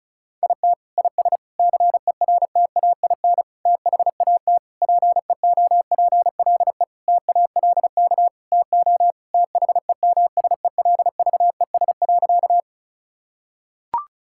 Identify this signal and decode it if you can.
Morse code, heard as a keyed tone, about 34 wpm, 700 Hz. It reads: IT IS CERTAIN THAT PEOPLE TALK TO THEMSELVES.
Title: Sentences from Top 1000 Words - Review Part 5 - 35wpm